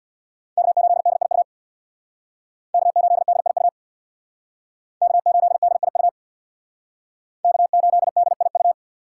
Morse code, heard as a keyed tone, about 63 wpm, 700 Hz. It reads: K8DIU K8DIU K8DIU K8DIU